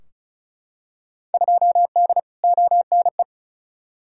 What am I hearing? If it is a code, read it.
Morse code, heard as a keyed tone, about 36 wpm, 700 Hz. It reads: 2D ONE
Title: Field Day Exchanges - 35wpm